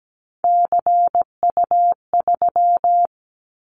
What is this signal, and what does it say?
Morse code, heard as a keyed tone, about 17 wpm, 700 Hz. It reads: CU3
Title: Callsign Prefixes for Top 100 DXCC - 17wpm